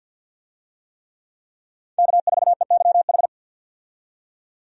Morse code, heard as a keyed tone, about 52 wpm, 700 Hz. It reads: K4EYH
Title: USA Call Signs - 5 Characters - 50wpm